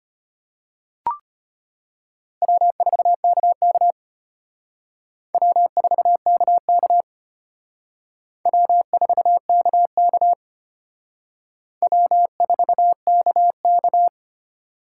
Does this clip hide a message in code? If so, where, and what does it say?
Yes: Morse code from 2.4 to 14.1 s, heard as a keyed tone, about 32 wpm, 700 Hz. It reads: W4KK W4KK W4KK W4KK